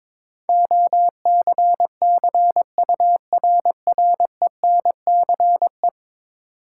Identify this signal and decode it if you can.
Morse code, heard as a keyed tone, about 22 wpm, 700 Hz. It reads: OCCURRENCE
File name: practice / 22wpm / 10 Letter Words - First 3 Letters Given - 22wpm